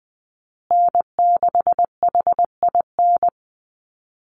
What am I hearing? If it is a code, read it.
Morse code, heard as a keyed tone, about 20 wpm, 700 Hz. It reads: N6HIN
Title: USA Call Signs Repeated 2x 20wpm